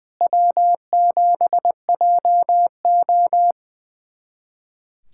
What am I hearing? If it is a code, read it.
Morse code, heard as a keyed tone, about 21 wpm, 700 Hz. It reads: W7JO
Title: USA Callsign DE Callsign - 4 Characters - 20wpm